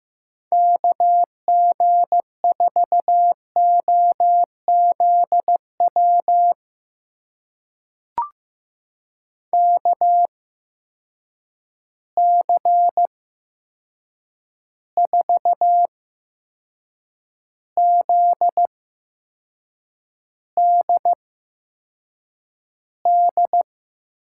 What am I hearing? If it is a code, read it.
Morse code, heard as a keyed tone, about 15 wpm, spaced as for 6 wpm, 700 Hz. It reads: KG4OZW KC4ZDD
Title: USA Call Signs - 6 Characters - 8x Character Spacing - 15wpm